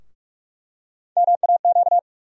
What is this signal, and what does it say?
Morse code, heard as a keyed tone, about 42 wpm, 700 Hz. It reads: MAQ
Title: USA Call Sign Postfixes - 3 Letters 45wpm